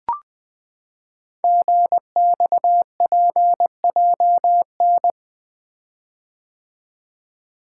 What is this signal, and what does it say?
Morse code, heard as a keyed tone, about 20 wpm, 700 Hz. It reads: GXPJN
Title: Code Groups - 5 Letter:Numbers - 20wpm